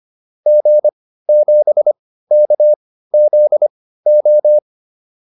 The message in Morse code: G7KZO

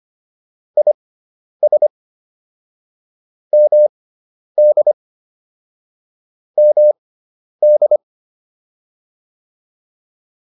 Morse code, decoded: IS MD MD

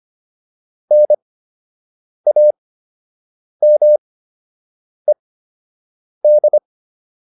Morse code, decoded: NAMED

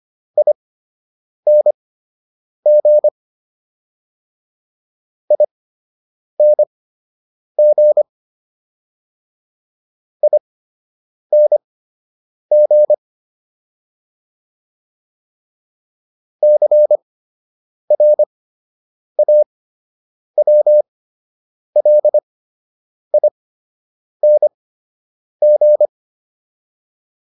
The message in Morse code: ING ING ING CRAWLING

